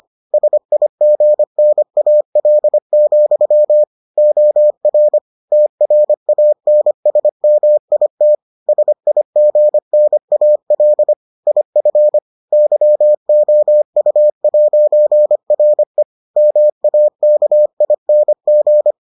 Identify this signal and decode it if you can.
Morse code, heard as a keyed tone, about 25 wpm, 600 Hz. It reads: SIGNAL, OR TRANSMIT SIGNAL IF YOU'RE MAKING